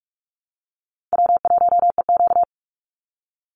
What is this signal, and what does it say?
Morse code, heard as a keyed tone, about 44 wpm, 700 Hz. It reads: W1EQ
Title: Call Signs Repeated 3x - 45wpm